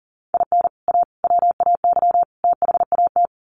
Morse code, heard as a keyed tone, about 40 wpm, 700 Hz. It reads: IN A WAY THAT
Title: Sets of 4 Words 40wpm